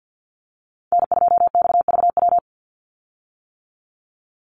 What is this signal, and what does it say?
Morse code, heard as a keyed tone, about 50 wpm, 700 Hz. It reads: N2XVW
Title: USA Call Signs 50wpm